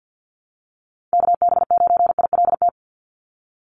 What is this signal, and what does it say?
Morse code, heard as a keyed tone, about 50 wpm, 700 Hz. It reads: KB9ILT
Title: NA SOTA Activator Callsigns 50wpm